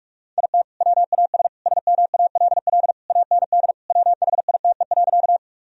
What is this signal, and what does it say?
Morse code, heard as a keyed tone, about 46 wpm, 700 Hz. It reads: IT WAS SMALL AND WHITE.
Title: Sentences from Top 1000 Words - Review Part 1 - 45wpm